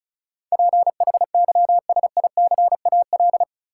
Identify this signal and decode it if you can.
Morse code, heard as a keyed tone, about 34 wpm, 700 Hz. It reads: PHYSICAL
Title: Sentences from Top 1000 Words - Review Part 4 - New Word Drill - 35wpm